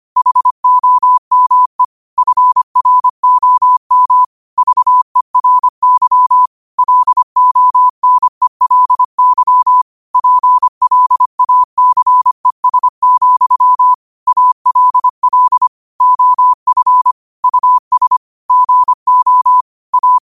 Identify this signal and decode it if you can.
Morse code, heard as a keyed tone, about 25 wpm, 1000 Hz. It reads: HOME FROM VERY LONELY PLACES, ALL OF US GO A
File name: qotdfiles / 25wpm